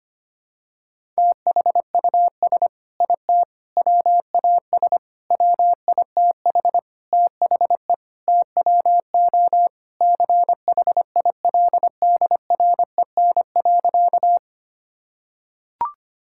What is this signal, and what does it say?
Morse code, heard as a keyed tone, about 25 wpm, 700 Hz. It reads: THUS IT WAS WITH THE TWO CHILDREN.
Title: Sentences from Top 1000 Words - Review Part 5 - 25wpm